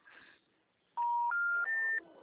Special information tone.